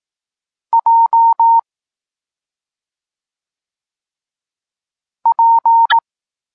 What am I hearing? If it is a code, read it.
Morse code, heard as a keyed tone, about 18 wpm, 920 Hz. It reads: J P
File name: Tekst-34 4